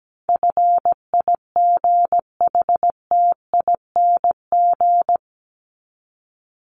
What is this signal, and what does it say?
Morse code, heard as a keyed tone, about 17 wpm, 700 Hz. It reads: FIGHTING
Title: ING Suffix - Top 100 Words with Base Words 17wpm